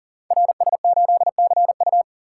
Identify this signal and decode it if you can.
Morse code, heard as a keyed tone, about 43 wpm, 700 Hz. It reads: RS8CU
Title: Code Groups - 5 Letter:Numbers - 40wpm